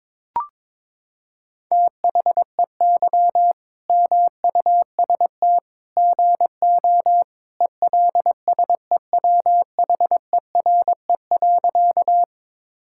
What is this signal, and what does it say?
Morse code, heard as a keyed tone, about 22 wpm, 700 Hz. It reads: THEY MUST GO ELSEWHERE.
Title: Sentences from Top 1600 Words 22wpm